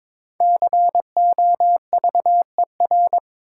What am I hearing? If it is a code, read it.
Morse code, heard as a keyed tone, about 22 wpm, 700 Hz. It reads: COVER
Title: Five Letter Words - 22wpm